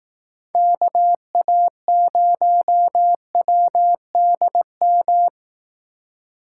Morse code, heard as a keyed tone, about 18 wpm, 700 Hz. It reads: KA0WDM
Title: Speed Racing - USA 2x3 Call Signs 15wpm